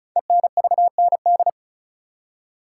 Morse code, heard as a keyed tone, about 36 wpm, 700 Hz. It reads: ENVND
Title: Code Groups - 5 Letters - 35wpm